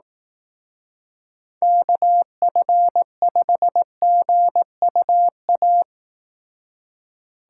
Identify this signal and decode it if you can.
Morse code, heard as a keyed tone, about 18 wpm, 700 Hz. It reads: KF5GUA
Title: Speed Racing - USA Call Signs 15wpm